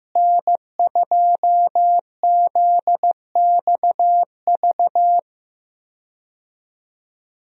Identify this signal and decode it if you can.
Morse code, heard as a keyed tone, about 15 wpm, 700 Hz. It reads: N2ZXV
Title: USA Call Signs - 5 Characters - Repeated 3x - 15wpm